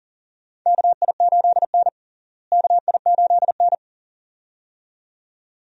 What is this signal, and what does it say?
Morse code, heard as a keyed tone, about 40 wpm, 700 Hz. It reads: KI8N KI8N
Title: USA Call Signs - 4 Characters - Repeated 2x - 40wpm